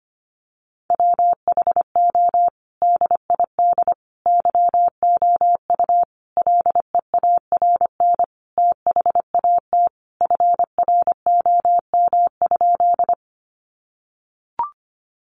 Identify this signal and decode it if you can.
Morse code, heard as a keyed tone, about 25 wpm, 700 Hz. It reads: WHO DID YOU LEARN THAT FROM?
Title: Sentences from Top 300 Words 25wpm